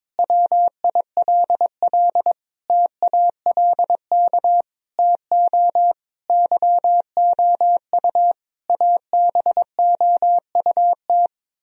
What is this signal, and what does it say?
Morse code, heard as a keyed tone, about 21 wpm, 700 Hz. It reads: WILL TALK TO YOU ABOUT
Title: Sentences from Top 1000 Words - Review Part 1 - 22wpm